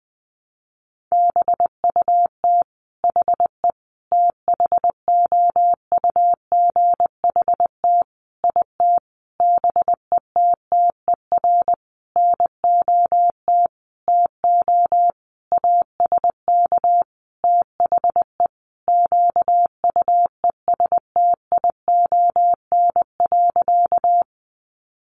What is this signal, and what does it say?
Morse code, heard as a keyed tone, about 20 wpm, 700 Hz. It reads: BUT HE THOUGHT IT BETTER NOT TO ASK THE QUESTION.
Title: Sentences from Top 600 Words 20wpm